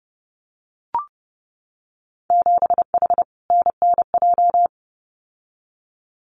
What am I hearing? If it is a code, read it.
Morse code, heard as a keyed tone, about 30 wpm, 700 Hz. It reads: 7H NNJ
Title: Winter Field Day Exchanges Encore 30wpm